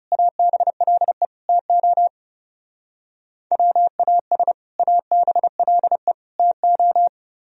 Morse code, heard as a keyed tone, about 37 wpm, 700 Hz. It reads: ABLE TO WAS ABLE TO
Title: Speed-Racing - Sets of 3 Words 30wpm